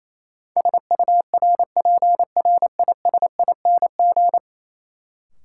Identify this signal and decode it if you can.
Morse code, heard as a keyed tone, about 28 wpm, 700 Hz. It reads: SURPRISING
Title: Sets of 5 Words - New Word Drill - 28wpm